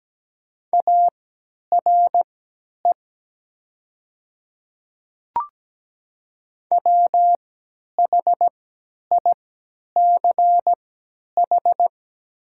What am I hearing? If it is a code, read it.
Morse code, heard as a keyed tone, about 17 wpm, spaced as for 10 wpm, 700 Hz. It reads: ARE WHICH
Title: 3x Character Spacing - Top 300 Words - 17wpm